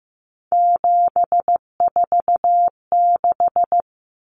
Morse code, heard as a keyed tone, about 15 wpm, 700 Hz. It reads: 746
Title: Common Cut Numbers - 3-Digit 15wpm